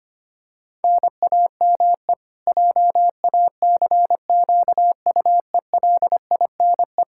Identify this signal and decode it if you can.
Morse code, heard as a keyed tone, about 25 wpm, 700 Hz. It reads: NAME JACQUELINE
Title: Practice QSO - Exchange 1 - Beginning - 25wpm